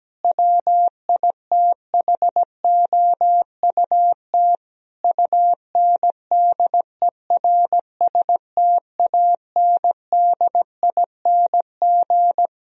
Morse code, heard as a keyed tone, about 17 wpm, 700 Hz. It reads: WITHOUT UNDERSTANDING
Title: Sentences from Top 1000 Words - Review Part 3 - 17wpm